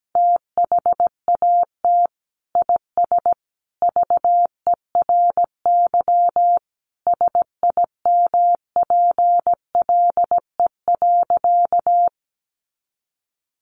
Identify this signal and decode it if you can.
Morse code, heard as a keyed tone, about 17 wpm, 700 Hz. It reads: THAT IS VERY SIMPLE.
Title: Sentences from Top 400 Words 17wpm